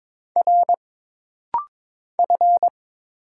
Morse code, heard as a keyed tone, about 22 wpm, 700 Hz. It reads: R F
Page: This rapid-fire timing is geared to push your ICR (Instant Character Recognition) skills.